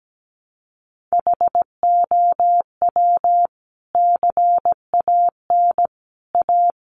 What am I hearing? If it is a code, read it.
Morse code, heard as a keyed tone, about 17 wpm, 700 Hz. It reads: HOW CAN A
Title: Sentences from Top 500 Words 17wpm